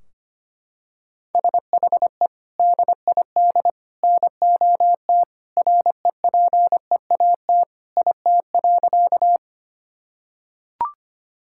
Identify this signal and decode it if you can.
Morse code, heard as a keyed tone, about 25 wpm, 700 Hz. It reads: SHE DID NOT REPEAT IT.
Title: Sentences from Top 1600 Words 25wpm